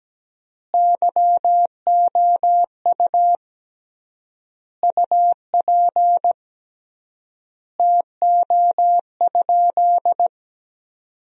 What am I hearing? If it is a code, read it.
Morse code, heard as a keyed tone, about 17 wpm, 700 Hz. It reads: YOU UP TO?
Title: 3x Word-Spacing - Sentences from Top 100 Words - 17wpm